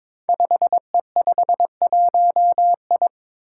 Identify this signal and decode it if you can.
Morse code, heard as a keyed tone, about 22 wpm, 700 Hz. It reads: 5E51I
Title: Five Challenging Letter-Number Combinations 22wpm